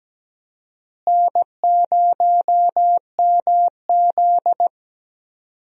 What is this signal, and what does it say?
Morse code, heard as a keyed tone, about 17 wpm, 700 Hz. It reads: N0MZ
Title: USA Call Signs - 4 Characters - Repeated 2x - 17wpm